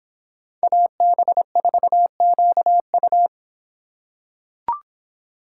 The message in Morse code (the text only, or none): AB4QU